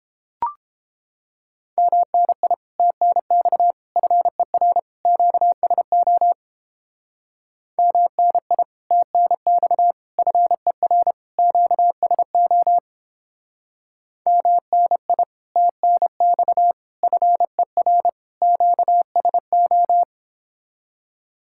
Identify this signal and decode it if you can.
Morse code, heard as a keyed tone, about 29 wpm, 700 Hz. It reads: MNI TNX FER QSO MNI TNX FER QSO MNI TNX FER QSO